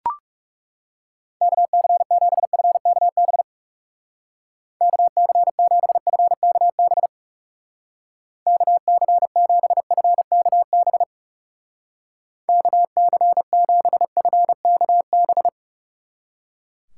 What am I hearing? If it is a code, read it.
Morse code, heard as a keyed tone, about 36 wpm, 700 Hz. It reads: KC7FKB KC7FKB KC7FKB KC7FKB
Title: Speed Racing - USA Call Signs 30wpm